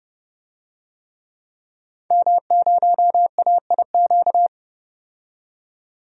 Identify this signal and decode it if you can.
Morse code, heard as a keyed tone, about 30 wpm, 700 Hz. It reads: M0AIQ
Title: DXCC-100 Call Signs - 5 Characters - 30wpm